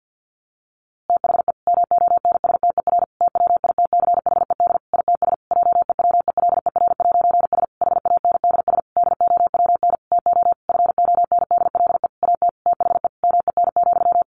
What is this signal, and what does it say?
Morse code, heard as a keyed tone, about 50 wpm, 700 Hz. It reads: THE MONSTER TWITCHED ITS JEWELER'S HANDS DOWN TO FONDLE AT THE MEN,
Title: A Sound Of Thunder 50wpm